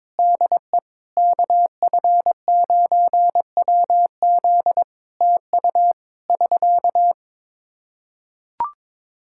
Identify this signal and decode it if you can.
Morse code, heard as a keyed tone, about 23 wpm, 700 Hz. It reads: DE KF9WZ TU <SK>
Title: K1USN SST QSO Practice 22wpm